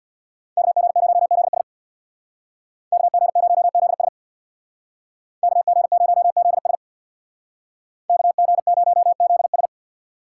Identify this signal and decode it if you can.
Morse code, heard as a keyed tone, about 62 wpm, 700 Hz. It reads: KG0ZS KG0ZS KG0ZS KG0ZS
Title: Speed Racing - USA Call Signs 50wpm